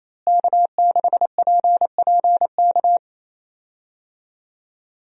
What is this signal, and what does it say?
Morse code, heard as a keyed tone, about 28 wpm, 700 Hz. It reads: K6PPK
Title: USA Call Signs - 5 Characters - Repeated 3x - 28wpm